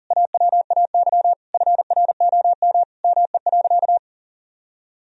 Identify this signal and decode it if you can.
Morse code, heard as a keyed tone, about 37 wpm, 700 Hz. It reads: AWAY FROM ME.
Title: Sentences from Top 1000 Words - Review Part 2 - 40wpm